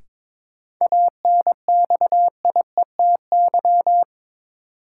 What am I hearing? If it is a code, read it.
Morse code, heard as a keyed tone, about 22 wpm, 700 Hz. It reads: ANXIETY
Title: Sentences from Top 1600 Words - New Word Drill - 22wpm